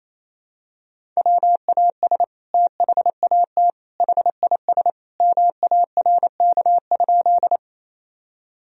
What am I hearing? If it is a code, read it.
Morse code, heard as a keyed tone, about 28 wpm, 700 Hz. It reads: WAS THAT HIS MARK?
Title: Sentences from Top 1000 Words - Review Part 1 - 28wpm